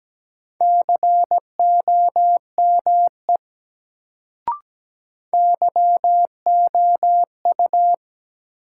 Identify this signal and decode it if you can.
Morse code, heard as a keyed tone, about 17 wpm, 700 Hz. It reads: COME YOU
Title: Sentences from Top 100 Words - New Word Drill - 17wpm